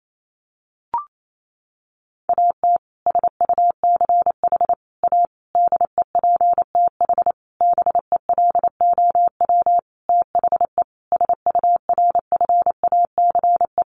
Morse code, heard as a keyed tone, about 28 wpm, 700 Hz. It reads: AT SUCH A DEPTH BELOW THE SURFACE
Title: Sentences from Top 1200 Words 28wpm